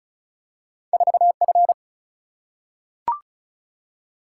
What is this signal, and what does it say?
Morse code, heard as a keyed tone, about 36 wpm, 700 Hz. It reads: 4F
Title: Single-Letter Single-Number 35wpm